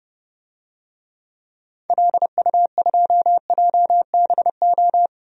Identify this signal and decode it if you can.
Morse code, heard as a keyed tone, about 32 wpm, 700 Hz. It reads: LU2JBO
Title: DXCC-100 Call Signs - 6 Characters - 30wpm